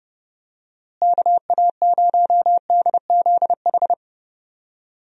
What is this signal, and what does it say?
Morse code, heard as a keyed tone, about 32 wpm, 700 Hz. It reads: KA0DZH
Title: USA Call Signs - 6 Characters - Repeated 3x - 30wpm